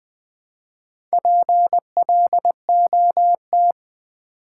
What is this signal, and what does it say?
Morse code, heard as a keyed tone, about 20 wpm, 700 Hz. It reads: PLOT
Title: Build A Word - 1 Letter At A Time - 3 to 5-Letter Words - 20wpm